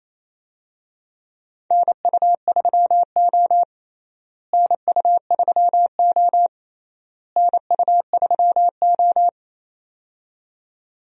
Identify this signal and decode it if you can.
Morse code, heard as a keyed tone, about 27 wpm, 700 Hz. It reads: NU3O NU3O NU3O
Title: USA Call Signs - 4 Characters - Repeated 3x - 28wpm